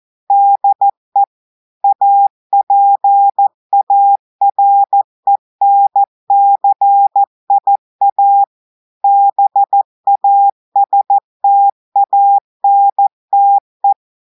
Morse code, heard as a keyed tone, about 14 wpm, 800 Hz. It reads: DE APARENCIA BASTANTE
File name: audios / portugais / portugais texte en clair 1 vitesse 14 mots minutes